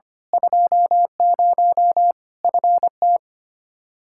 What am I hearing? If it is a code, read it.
Morse code, heard as a keyed tone, about 25 wpm, 700 Hz. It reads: 20 FT
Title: Practice QSO - Exchange 2 - Info - 25wpm